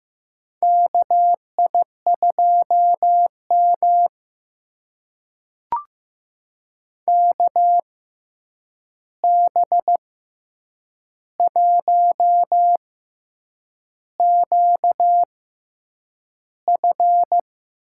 Morse code, heard as a keyed tone, about 15 wpm, spaced as for 7 wpm, 700 Hz. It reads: KI2M KB1QF
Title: USA Call Signs - 6x Character Spacing - 15wpm